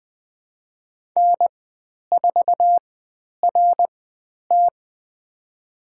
Morse code, heard as a keyed tone, about 20 wpm, spaced as for 10 wpm, 700 Hz. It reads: N4RT
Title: 095 Lesson 16g - Letter c - Call Signs 20wpm